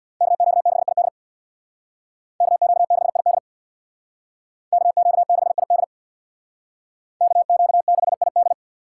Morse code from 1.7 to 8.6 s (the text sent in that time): KQ6ID KQ6ID KQ6ID